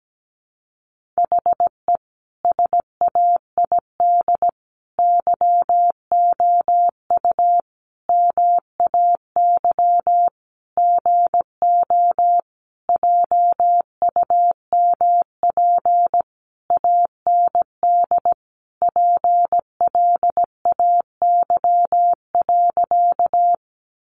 Morse code, heard as a keyed tone, about 17 wpm, 700 Hz. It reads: HE SAID YOU MAY GO JUMP AND PLAY.
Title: Sentences from Top 1300 Words 17wpm